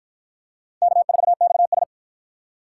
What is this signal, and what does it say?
Morse code, heard as a keyed tone, about 46 wpm, 700 Hz. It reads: K4XS